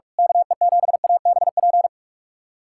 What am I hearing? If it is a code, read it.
Morse code, heard as a keyed tone, about 43 wpm, 700 Hz. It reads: KE7ABP
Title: USA Callsign DE Callsign - 6 Characters - 45wpm